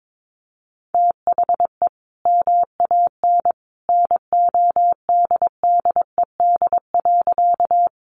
Morse code, heard as a keyed tone, about 22 wpm, 700 Hz. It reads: THE MAN NODDED.
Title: A Sound Of Thunder 22wpm